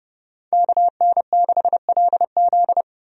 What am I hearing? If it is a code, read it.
Morse code, heard as a keyed tone, about 30 wpm, 700 Hz. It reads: KN6LZ
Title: USA Call Signs - 5 Characters - 30wpm